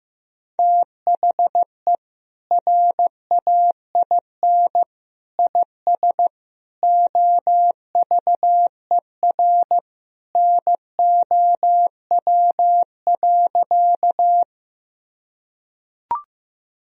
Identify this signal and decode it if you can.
Morse code, heard as a keyed tone, about 15 wpm, 700 Hz. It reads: THE RAIN IS OVER NOW.
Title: Sentences from Top 1000 Words - Review Part 1 - 15wpm